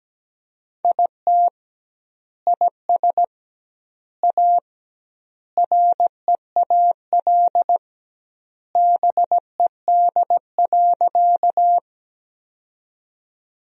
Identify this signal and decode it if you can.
Morse code, heard as a keyed tone, about 17 wpm, 700 Hz. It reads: IT IS A REAL BED.
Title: 2x Word-Spacing - Sentences from Top 500 Words - 17wpm